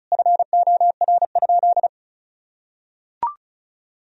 Morse code, heard as a keyed tone, about 36 wpm, 700 Hz. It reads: FOR?
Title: Sentences from Top 1000 Words - Review Part 1 - 35wpm